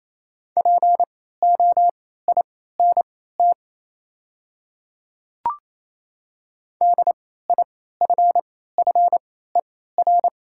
Morse code, heard as a keyed tone, about 28 wpm, spaced as for 16 wpm, 700 Hz. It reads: POINT DIFFER
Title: 3x Character Spacing - Top 200 Words - 28wpm